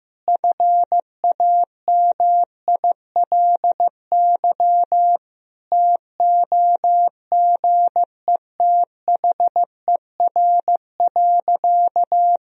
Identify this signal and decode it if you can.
Morse code, heard as a keyed tone, about 15 wpm, 700 Hz. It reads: FAMILY TOGETHER.
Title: Sentences from Top 1000 Words - Review Part 4 - 15wpm